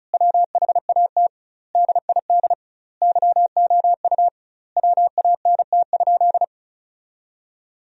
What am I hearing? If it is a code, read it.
Morse code, heard as a keyed tone, about 36 wpm, 700 Hz. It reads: WHAT DID YOU WANT?
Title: 2x Word-Spacing - Sentences from Top 200 Words - 35wpm